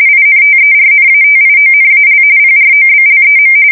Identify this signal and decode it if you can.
RTTY, about 45.45 baud. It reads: INTERNATIONAL PROBLEM
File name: fsk170 / usnavy2